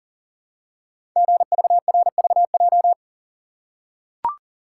Morse code, heard as a keyed tone, about 40 wpm, 700 Hz. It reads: GVRVJ
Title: Code Groups - 5 Letter:Numbers - 40wpm